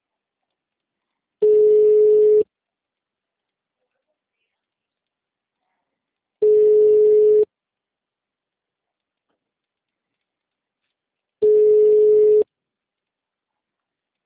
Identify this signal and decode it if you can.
Ringback tone (Europe, 425 Hz).
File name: out- -2041- - - .18115